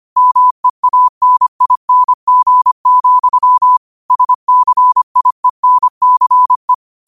Morse code, heard as a keyed tone, about 25 wpm, 1000 Hz. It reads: MEANING, SCIENCE